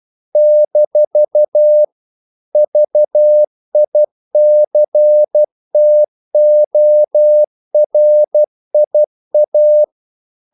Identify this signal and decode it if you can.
Morse code, heard as a keyed tone, about 12 wpm, 600 Hz. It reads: - VICTORIA
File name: CW / Practice / NationalCapitals12